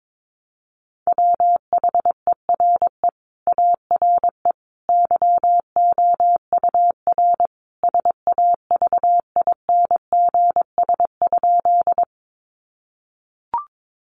Morse code, heard as a keyed tone, about 22 wpm, 700 Hz. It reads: WHERE ARE YOUR SAVINGS?
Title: Sentences from Top 1200 Words 22wpm